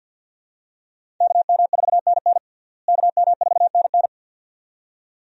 Morse code, heard as a keyed tone, about 46 wpm, 700 Hz. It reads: KM4NN KM4NN
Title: USA Call Signs - 5 Characters - Repeated 2x - 50wpm